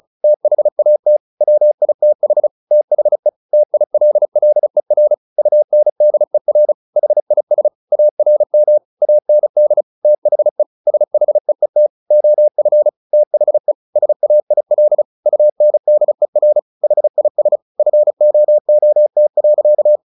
Morse code, heard as a keyed tone, about 34 wpm, 600 Hz. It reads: THAT WITH THE TILLER UNDER HIS ARM AND THE SHEET OF THE SAIL UNDER HIS FOOT.
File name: the old man and the sea 2 35wpm